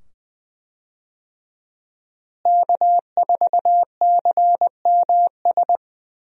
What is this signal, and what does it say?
Morse code, heard as a keyed tone, about 20 wpm, 700 Hz. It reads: K4CMS
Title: USA Call Signs - 5 Characters - Repeated 2x - 20wpm